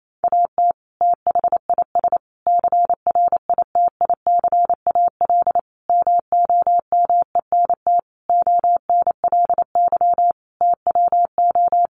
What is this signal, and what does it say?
Morse code, heard as a keyed tone, about 28 wpm, 700 Hz. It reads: AT THIS CRITICAL MOMENT ONLY TWO
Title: Sentences from Top 1100 Words 28wpm